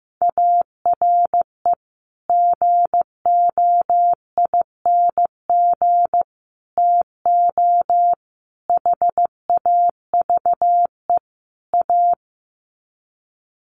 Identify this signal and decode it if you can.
Morse code, heard as a keyed tone, about 14 wpm, 700 Hz. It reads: ARE GOING TO HAVE A
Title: Sets of 5 Words Encore2 15wpm